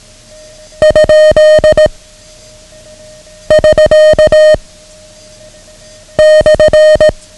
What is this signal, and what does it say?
Morse code, heard as a keyed tone, about 18 wpm, 620 Hz. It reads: ? <SK> /
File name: ham radio / CodeQuick / Code Quick 09